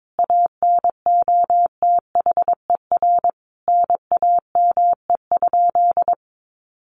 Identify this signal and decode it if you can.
Morse code, heard as a keyed tone, about 22 wpm, 700 Hz. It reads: ANOTHER NAME?
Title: Sentences from Top 600 Words 22wpm